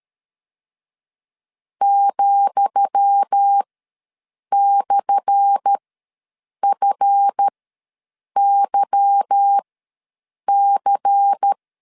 Morse code, heard as a keyed tone, about 13 wpm, spaced as for 7 wpm, 800 Hz. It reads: ,/FYC
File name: LEZIONI / 10 SIGNS / 10D COMA ws 12wpm 3618